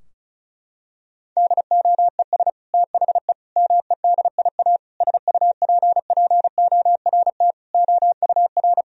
Morse code, heard as a keyed tone, about 34 wpm, 700 Hz. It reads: DOES THE MEDIA SUPPORT OUR
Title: Sentences from Top 1000 Words - Review Part 4 - 35wpm